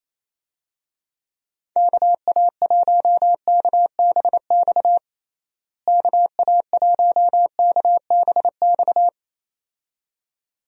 Morse code, heard as a keyed tone, about 28 wpm, 700 Hz. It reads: KA1KBX KA1KBX
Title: USA Call Signs - 6 Characters - Repeated 2x - 28wpm